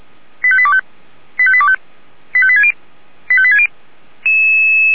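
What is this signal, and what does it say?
Five-tone selective call (ZVEI), heard as quick runs of five tones, tones of about 70 ms. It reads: 85825 85825 85860 85860
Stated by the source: Decoded by SDR as ZVEI five-tone selective call.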